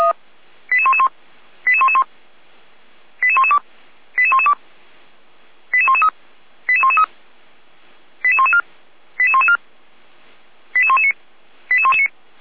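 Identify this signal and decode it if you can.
Five-tone selective call (ZVEI), heard as quick runs of five tones, tones of about 70 ms. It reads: 88111 88111 88112 88112 88113 88113 88114 88114 88119 88119
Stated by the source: ZVEI five-tone selective call received by SDR for feuerwehr-orlamuende (fire brigade Orlamünde).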